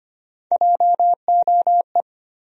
Morse code, heard as a keyed tone, about 25 wpm, 700 Hz. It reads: JOE
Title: USA Common Names - Max 3 Letters - 25wpm